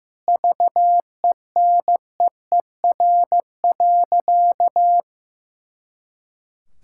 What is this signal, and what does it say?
Morse code, heard as a keyed tone, about 16 wpm, 700 Hz. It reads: VENEER.
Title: Pangram Challenge v2 - 15wpm